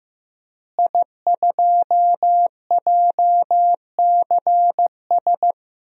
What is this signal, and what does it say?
Morse code, heard as a keyed tone, about 16 wpm, 700 Hz. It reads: I2JCS
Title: DXCC-100 Call Signs - 5 Characters - Repeated 2x - 15wpm